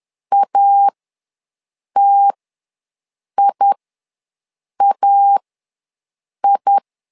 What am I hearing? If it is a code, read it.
Morse code, heard as a keyed tone, about 11 wpm, spaced as for 6 wpm, 800 Hz. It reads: ATIAI
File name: LEZIONI / 2 UHDN / 2 U ws 10wpm 2017